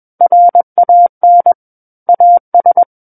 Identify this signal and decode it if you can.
Morse code, heard as a keyed tone, about 21 wpm, 700 Hz. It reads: RAN AS
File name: Book One / 12-Chapter 12 - 21 WPM